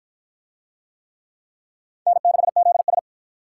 Morse code, heard as a keyed tone, about 53 wpm, 700 Hz. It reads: N6ZS